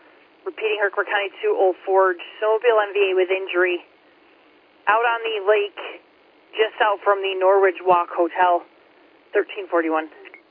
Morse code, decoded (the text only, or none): none